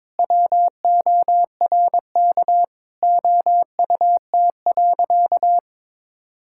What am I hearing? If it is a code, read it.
Morse code, heard as a keyed tone, about 21 wpm, 700 Hz. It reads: WORK OUT.
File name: All-Age Appropriate Jokes - 22wpm